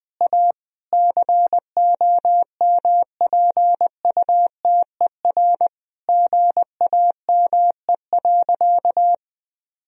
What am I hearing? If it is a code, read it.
Morse code, heard as a keyed tone, about 21 wpm, 700 Hz. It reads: A COMPUTER GAME.
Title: Sentences from Top 1000 Words - Review Part 3 - 20wpm